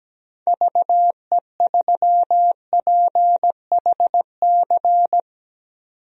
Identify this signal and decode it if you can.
Morse code, heard as a keyed tone, about 17 wpm, 700 Hz. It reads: VE3PHC
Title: DXCC-100 Call Signs - 6 Characters - 17wpm